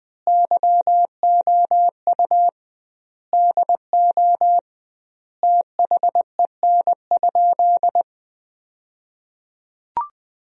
Morse code, heard as a keyed tone, about 21 wpm, 700 Hz. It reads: YOU DO THEN?
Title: 2x Word-Spacing - Sentences from Top 100 Words - 20wpm